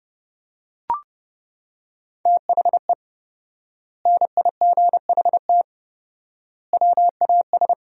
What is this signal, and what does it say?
Morse code, heard as a keyed tone, about 30 wpm, 700 Hz. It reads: THE NIGHT WAS
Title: 4x Word-Spacing - Sentences from Top 500 Words - 30wpm